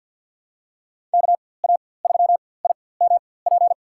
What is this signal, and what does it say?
Morse code, heard as a keyed tone, about 51 wpm, spaced as for 23 wpm, 700 Hz. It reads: KA3IMP
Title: USA Call Signs - 6 Characters - 4x Character Spacing - 50wpm